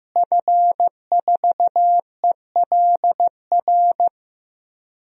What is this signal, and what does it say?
Morse code, heard as a keyed tone, about 15 wpm, 700 Hz. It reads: F4ELR
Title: DXCC-100 Call Signs - 5 Characters - Repeated 3x - 15wpm